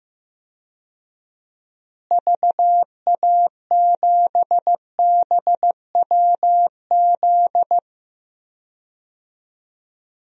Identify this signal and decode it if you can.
Morse code, heard as a keyed tone, about 15 wpm, 700 Hz. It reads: VA7BWZ
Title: DXCC-100 Call Signs - 6 Characters - Repeated 2x - 15wpm